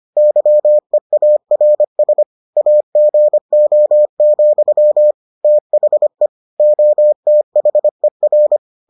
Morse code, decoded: YEARS AGO, THE OTHER

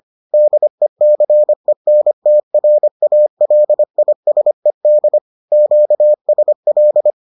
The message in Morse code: DECENTRALISED QSL

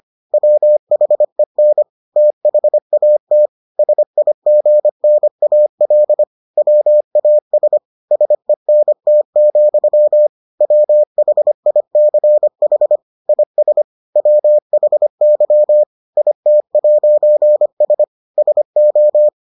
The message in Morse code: WHEN THAT SIGNAL WAS SENT, WHICH IS WHY IT'S SO

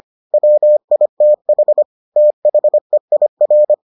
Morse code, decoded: WITH THEIR

